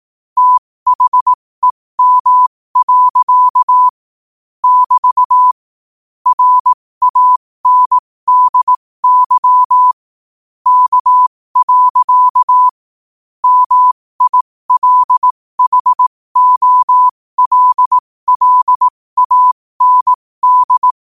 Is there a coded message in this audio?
Yes: Morse code from 0.4 to 20.9 s, heard as a keyed tone, about 18 wpm, spaced as for 15 wpm, 1000 Hz. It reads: THEM. <BT> RANDY K. MILHOLLAND